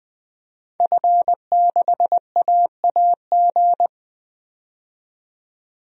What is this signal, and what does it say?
Morse code, heard as a keyed tone, about 20 wpm, 700 Hz. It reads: F6AAG
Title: DXCC-100 Call Signs - 5 Characters - 20wpm